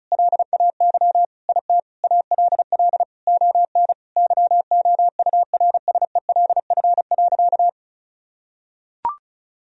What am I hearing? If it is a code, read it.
Morse code, heard as a keyed tone, about 34 wpm, 700 Hz. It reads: LAY IT ALL ON YOURSELF.
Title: Sentences from Top 1000 Words - Review Part 2 - 35wpm